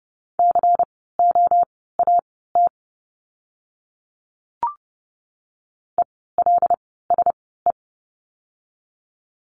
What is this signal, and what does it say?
Morse code, heard as a keyed tone, about 30 wpm, spaced as for 17 wpm, 700 Hz. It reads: COAT ELSE
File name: Lesson 16f - Letter c - Words 30wpm